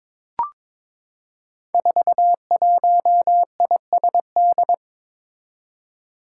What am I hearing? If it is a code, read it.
Morse code, heard as a keyed tone, about 22 wpm, 700 Hz. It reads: 41ISD
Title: Five Challenging Letter-Number Combinations 22wpm